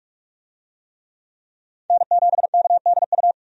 Morse code, heard as a keyed tone, about 45 wpm, 700 Hz. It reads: N7KDU
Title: USA Call Signs - 5 Characters - 45wpm